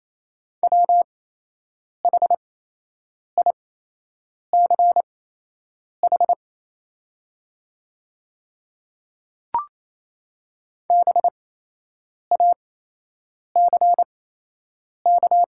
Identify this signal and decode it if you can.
Morse code, heard as a keyed tone, about 28 wpm, spaced as for 8 wpm, 690 Hz. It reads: WHICH BACK
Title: 8x Character Spacing - Top 300 Words - 28wpm